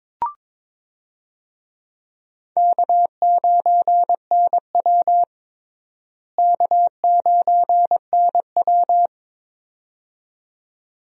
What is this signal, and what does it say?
Morse code, heard as a keyed tone, about 23 wpm, 700 Hz. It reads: K9NW K9NW
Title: USA Call Signs - 4 Characters - Repeated 2x - 22wpm